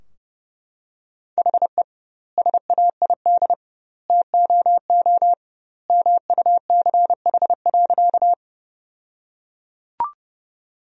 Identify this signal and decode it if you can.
Morse code, heard as a keyed tone, about 32 wpm, 700 Hz. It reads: HE SAID TOO MUCH.
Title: 2x Word-Spacing - Sentences from Top 200 Words - 30wpm